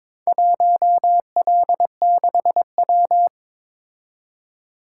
Morse code, heard as a keyed tone, about 22 wpm, 700 Hz. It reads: 1L6W
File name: Four Letter-Number Combinations - 22wpm